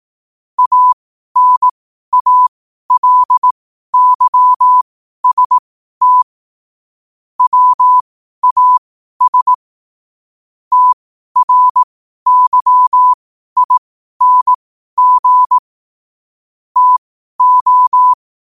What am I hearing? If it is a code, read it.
Morse code, heard as a keyed tone, about 18 wpm, spaced as for 12 wpm, 1000 Hz. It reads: ANALYST WAS TRYING TO
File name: qotdfiles / 12wpm